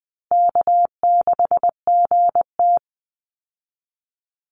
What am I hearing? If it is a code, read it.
Morse code, heard as a keyed tone, about 20 wpm, 700 Hz. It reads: K6GT
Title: Easy Call Signs Repeated 3x - 20wpm